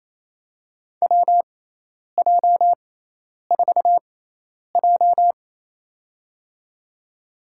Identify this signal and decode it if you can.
Morse code, heard as a keyed tone, about 28 wpm, spaced as for 10 wpm, 700 Hz. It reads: WJ4J
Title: USA Call Signs - 4 Characters - 6x Character Spacing - 28wpm